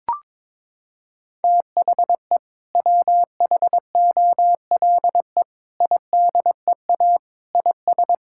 Morse code, heard as a keyed tone, about 22 wpm, 700 Hz. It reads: THE WHOLE IDEA IS
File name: practice / 22wpm / Sentences with 2-Word Phrases - 22wpm